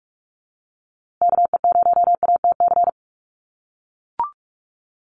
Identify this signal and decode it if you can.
Morse code, heard as a keyed tone, about 44 wpm, 690 Hz. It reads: KE0ATC
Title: USA Call Signs Repeated 3x 45wpm